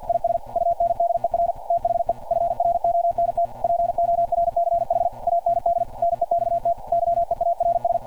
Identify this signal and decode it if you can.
Morse code, heard as a keyed tone, about 46 wpm, 690 Hz. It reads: IS UNDER THE FRONT PORCH AND NEXT YEARS
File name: archived / CRSnet